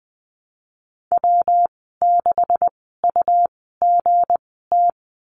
Morse code, heard as a keyed tone, about 20 wpm, spaced as for 15 wpm, 700 Hz. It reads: W6UGT
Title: 207 Lesson 33g - Letter j - Call Signs 20wpm